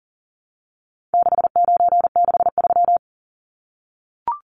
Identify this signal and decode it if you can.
Morse code, heard as a keyed tone, about 40 wpm, 700 Hz. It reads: 6963
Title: Numbers - 4-Digit 40wpm